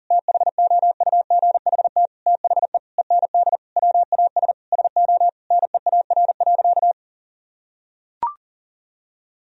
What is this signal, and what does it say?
Morse code, heard as a keyed tone, about 37 wpm, 700 Hz. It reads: THOUGHT THE END WAS SO NEAR.
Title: Sentences from Top 1000 Words - Review Part 1 - 40wpm